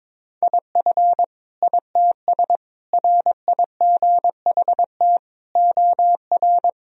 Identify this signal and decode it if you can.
Morse code, heard as a keyed tone, about 22 wpm, 700 Hz. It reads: IF ITS RIGHT OR
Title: Sentences from Top 1000 Words - Review Part 3 - 22wpm